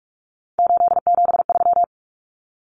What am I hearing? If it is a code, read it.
Morse code, heard as a keyed tone, about 44 wpm, 700 Hz. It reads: 873
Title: Common Cut Numbers - 3-Digit 45wpm